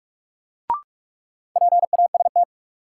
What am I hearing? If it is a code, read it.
Morse code, heard as a keyed tone, about 43 wpm, 700 Hz. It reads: PAST
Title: Sentences from Top 600 Words - New Word Drill - 45wpm